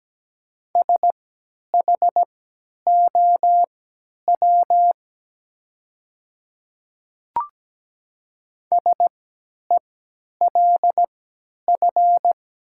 Morse code, heard as a keyed tone, about 17 wpm, spaced as for 10 wpm, 700 Hz. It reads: SHOW SELF
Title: 3x Character Spacing - Top 200 Words - 17wpm